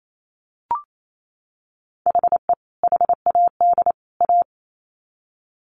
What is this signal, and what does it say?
Morse code, heard as a keyed tone, about 28 wpm, 700 Hz. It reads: HE HAD A
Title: Sets of 3 Words 28wpm